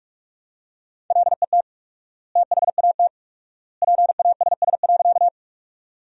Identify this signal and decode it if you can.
Morse code, heard as a keyed tone, about 46 wpm, 700 Hz. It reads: LET THAT PASS.
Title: 4x Word-Spacing - Sentences from Top 400 Words - 45wpm